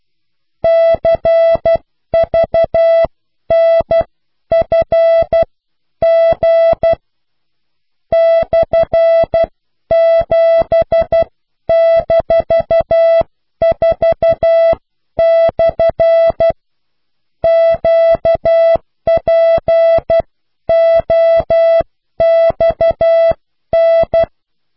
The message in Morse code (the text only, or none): CVNFG /7-4/ QPOXN